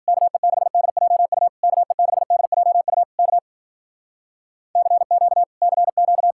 Morse code, heard as a keyed tone, about 52 wpm, 700 Hz. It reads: KE6DJU KE6DJU K CQ CQ